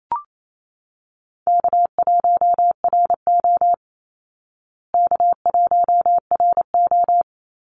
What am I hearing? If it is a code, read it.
Morse code, heard as a keyed tone, about 28 wpm, 700 Hz. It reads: K1RO K1RO
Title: Easy Call Signs Repeated 2x - 28wpm